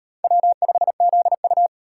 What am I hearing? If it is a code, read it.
Morse code, heard as a keyed tone, about 35 wpm, 700 Hz. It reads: W5ZU